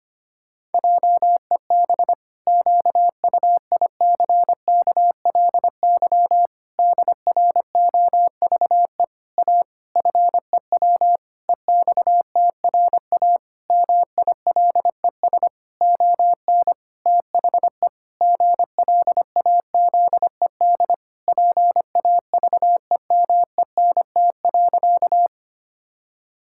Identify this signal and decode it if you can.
Morse code, heard as a keyed tone, about 25 wpm, 700 Hz. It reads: JEB QUICKLY DROVE A FEW EXTRA MILES ON THE GLAZED PAVEMENT.
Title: Pangram Challenge v2 - 25wpm